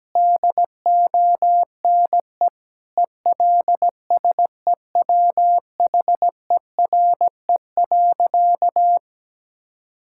Morse code, heard as a keyed tone, about 17 wpm, 700 Hz. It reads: DONE ELSEWHERE.
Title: Sentences from Top 1600 Words 17wpm